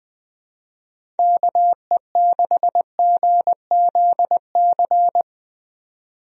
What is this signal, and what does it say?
Morse code, heard as a keyed tone, about 20 wpm, 700 Hz. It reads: KE6GZC
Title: USA Call Signs - 6 Characters - 20wpm